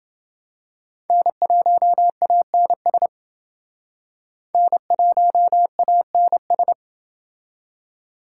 Morse code, heard as a keyed tone, about 29 wpm, 700 Hz. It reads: N1ANS N1ANS